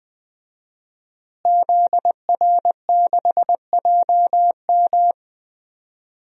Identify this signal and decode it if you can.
Morse code, heard as a keyed tone, about 20 wpm, 700 Hz. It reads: ZR6JM
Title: DXCC-100 Call Signs - 5 Characters - Repeated 2x - 20wpm